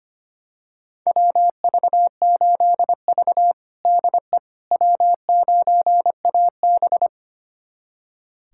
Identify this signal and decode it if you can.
Morse code, heard as a keyed tone, about 25 wpm, 700 Hz. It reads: WV8V DE W9AB
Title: USA Callsign DE Callsign - 4 Characters - 25wpm